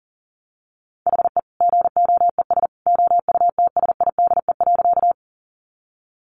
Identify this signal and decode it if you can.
Morse code, heard as a keyed tone, about 40 wpm, 700 Hz. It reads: HE GOES OUTSIDE.
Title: Additional Easy Phrases 40wpm